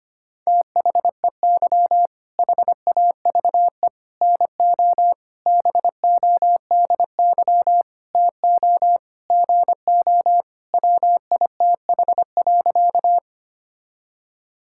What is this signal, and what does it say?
Morse code, heard as a keyed tone, about 25 wpm, 700 Hz. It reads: THEY HAVE NO BODY TO GO WITH.
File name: practice / 25wpm / All-Age Appropriate Jokes - 25wpm